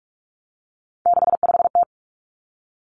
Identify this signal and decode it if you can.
Morse code, heard as a keyed tone, about 46 wpm, 700 Hz. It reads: 65T
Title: Three Challenging Letter-Number Combinations 45wpm